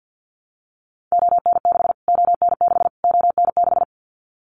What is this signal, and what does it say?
Morse code, heard as a keyed tone, about 50 wpm, 700 Hz. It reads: ON6 ON6 ON6
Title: Callsign Prefixes for Top 25 DXCC - Repeated 3x - 50wpm